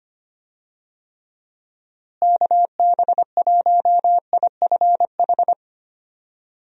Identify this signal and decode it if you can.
Morse code, heard as a keyed tone, about 25 wpm, 700 Hz. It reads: KB1IFH
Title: USA Call Signs - 6 Characters - Repeated 3x - 25wpm